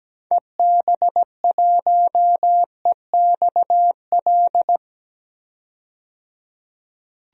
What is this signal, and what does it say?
Morse code, heard as a keyed tone, about 17 wpm, 700 Hz. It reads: EB1EXL
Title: DXCC-100 Call Signs - 6 Characters - 17wpm